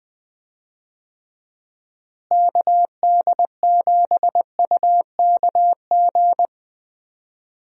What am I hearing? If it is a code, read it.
Morse code, heard as a keyed tone, about 20 wpm, 700 Hz. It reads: KD7UKG
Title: USA Call Signs - 6 Characters - Repeated 2x - 20wpm